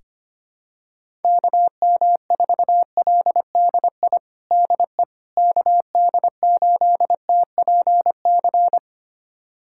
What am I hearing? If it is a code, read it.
Morse code, heard as a keyed tone, about 25 wpm, 700 Hz. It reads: KM4LDI DE KD8TPC
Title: USA Callsign DE Callsign - 6 Characters - 25wpm